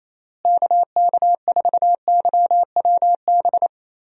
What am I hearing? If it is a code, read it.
Morse code, heard as a keyed tone, about 27 wpm, 700 Hz. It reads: KK4YWB
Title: USA Call Signs - 6 Characters - Repeated 3x - 28wpm